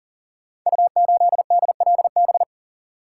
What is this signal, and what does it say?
Morse code, heard as a keyed tone, about 38 wpm, 700 Hz. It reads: U8DLB
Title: Code Groups - 5 Letter:Numbers - 40wpm